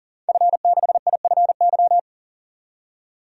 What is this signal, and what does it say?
Morse code, heard as a keyed tone, about 38 wpm, 700 Hz. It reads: F6IFY
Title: DXCC-100 Call Signs - 5 Characters - 40wpm